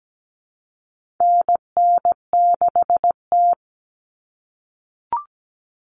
Morse code, heard as a keyed tone, about 17 wpm, 700 Hz. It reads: NN6T
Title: Call Signs - 17wpm